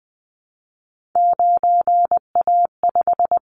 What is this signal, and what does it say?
Morse code, heard as a keyed tone, about 20 wpm, 700 Hz. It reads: 9A5
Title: Callsign Prefixes for Top 25 DXCC - 20wpm